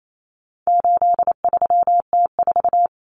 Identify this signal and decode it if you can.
Morse code, heard as a keyed tone, about 28 wpm, 700 Hz. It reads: 83T4
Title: Common Cut Numbers - 4-Digit 28wpm